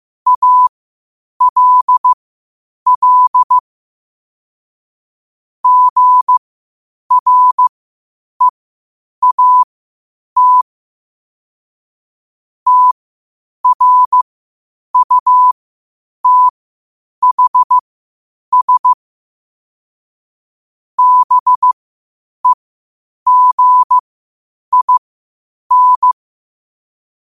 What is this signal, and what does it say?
Morse code, heard as a keyed tone, about 15 wpm, spaced as for 8 wpm, 1000 Hz. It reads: ALL GREAT TRUTHS BEGIN
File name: qotdfiles / 8wpm